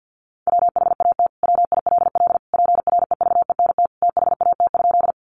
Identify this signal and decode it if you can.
Morse code, heard as a keyed tone, about 50 wpm, 700 Hz. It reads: WHAT WILL PREVENT THAT?
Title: Sentences from Top 1100 Words 50wpm